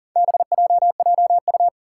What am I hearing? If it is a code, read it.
Morse code, heard as a keyed tone, about 40 wpm, 700 Hz. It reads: BJJU
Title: Four Challenging Letter-Number Combinations 40wpm